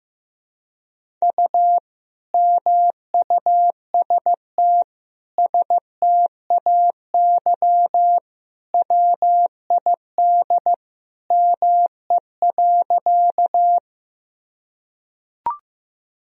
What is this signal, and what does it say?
Morse code, heard as a keyed tone, about 16 wpm, 700 Hz. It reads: U MUST STAY WID ME.
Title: Sentences from Top 500 Words and Abbreviations 15wpm